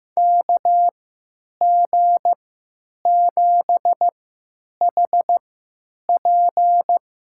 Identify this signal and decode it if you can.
Morse code, heard as a keyed tone, about 15 wpm, spaced as for 9 wpm, 700 Hz. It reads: KG7HP
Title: USA Call Signs - 3x Character Spacing - 15wpm